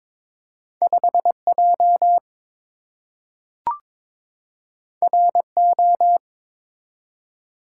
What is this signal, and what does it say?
Morse code, heard as a keyed tone, about 22 wpm, 700 Hz. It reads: 5J RO
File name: practice / 22wpm / Two Letter-Number Combinations - 22wpm